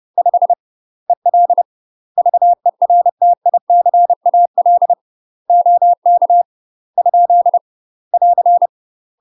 Morse code, heard as a keyed tone, about 30 wpm, 700 Hz. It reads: 5 EL VERTICAL OK ? <AR>
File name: UK MASTER QSOs TRAINING / 30WPM / UK MASTER QSOs 0048